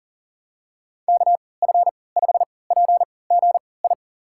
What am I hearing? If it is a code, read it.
Morse code, heard as a keyed tone, about 40 wpm, spaced as for 23 wpm, 700 Hz. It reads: KF5PGI
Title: USA Call Signs - 3x Character Spacing - 40wpm